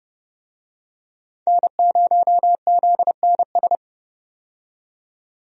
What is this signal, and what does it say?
Morse code, heard as a keyed tone, about 30 wpm, 700 Hz. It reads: N0ZNS
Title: USA Call Signs - 5 Characters - 30wpm